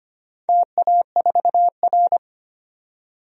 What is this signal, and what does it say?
Morse code, heard as a keyed tone, about 25 wpm, 700 Hz. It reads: TA4R
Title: DXCC-100 Call Signs - 4 Characters - 25wpm